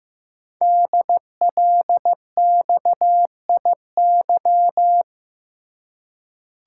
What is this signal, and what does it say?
Morse code, heard as a keyed tone, about 15 wpm, 700 Hz. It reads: DLXIY
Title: Code Groups - 5 Letters - 15wpm